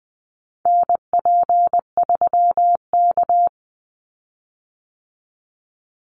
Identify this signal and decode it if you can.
Morse code, heard as a keyed tone, about 20 wpm, 700 Hz. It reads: NP3K
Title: Call Signs Repeated 3x - 20wpm